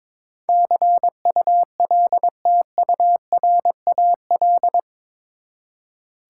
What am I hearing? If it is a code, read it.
Morse code, heard as a keyed tone, about 22 wpm, 700 Hz. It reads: CULTURAL
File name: practice / 22wpm / Eight Letter Words - 22wpm